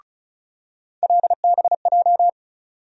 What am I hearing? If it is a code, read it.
Morse code, heard as a keyed tone, about 36 wpm, 700 Hz. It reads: LBJ
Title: USA Call Sign Postfixes - 3 Letters 35wpm